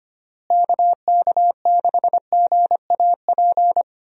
Morse code, heard as a keyed tone, about 25 wpm, 700 Hz. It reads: KK6GAP
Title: USA Call Signs - 6 Characters - Repeated 3x - 25wpm